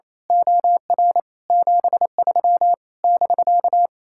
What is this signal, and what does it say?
Morse code, heard as a keyed tone, about 27 wpm, 700 Hz. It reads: OR 73 <BK>
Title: Speed-Racing Common QSO Phrases - 28wpm